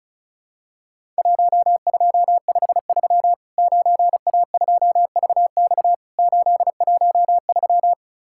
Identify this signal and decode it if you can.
Morse code, heard as a keyed tone, about 36 wpm, 700 Hz. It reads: 1253 9A2VX 813
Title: Worked All Europe DX Contest - QTC Exchange - 35wpm